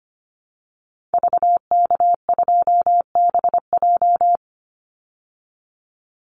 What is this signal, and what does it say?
Morse code, heard as a keyed tone, about 25 wpm, 700 Hz. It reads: VK2BJ
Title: Call Signs Repeated 2x - 25wpm